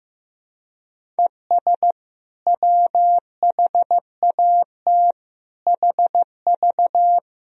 Morse code, heard as a keyed tone, about 15 wpm, 700 Hz. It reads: ES WHAT HV
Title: Sentences from Top 500 Words and Abbreviations 15wpm